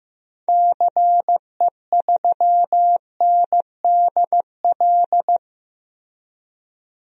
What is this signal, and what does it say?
Morse code, heard as a keyed tone, about 15 wpm, 700 Hz. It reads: CE3NDL
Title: DXCC-100 Call Signs - 6 Characters - 15wpm